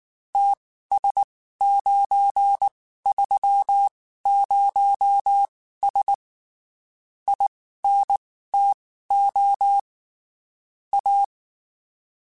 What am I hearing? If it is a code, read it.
Morse code, heard as a keyed tone, about 19 wpm, spaced as for 14 wpm, 800 Hz. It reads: TS930S INTO A
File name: (spaced) / QSOs Spaced-mp3s / WA7QOX 2 @18 sp